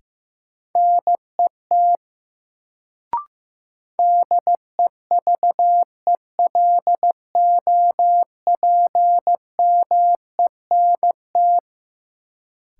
Morse code, heard as a keyed tone, about 16 wpm, 700 Hz. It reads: NET DEVELOPMENT
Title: Sentences from Top 1400 Words - New Word Drill Alt - 15wpm